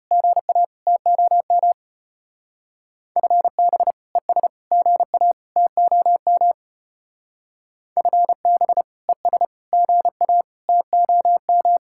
Morse code, heard as a keyed tone, about 33 wpm, 700 Hz. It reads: GA TOM FB ES GA TOM FB ES GA TOM